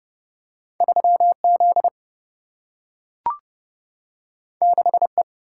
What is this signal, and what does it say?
Morse code, heard as a keyed tone, about 31 wpm, 700 Hz. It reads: 3Z 6E
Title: Single-Letter Single-Number 30wpm